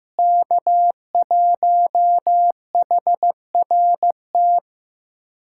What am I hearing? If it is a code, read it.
Morse code, heard as a keyed tone, about 15 wpm, 700 Hz. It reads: K1HRT
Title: USA Callsign DE Callsign - 5 Characters - 15wpm